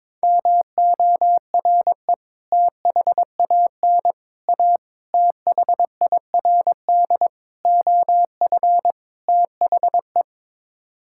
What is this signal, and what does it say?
Morse code, heard as a keyed tone, about 22 wpm, 700 Hz. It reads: MORE THAN A THIRD OF THE
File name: practice / 22wpm / Sets of 6 Words - 22wpm